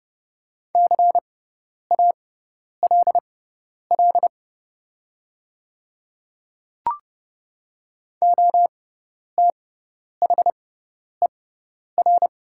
Morse code, heard as a keyed tone, about 30 wpm, spaced as for 10 wpm, 690 Hz. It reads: CALL OTHER
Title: 6x Character Spacing - Top 200 Words - 30wpm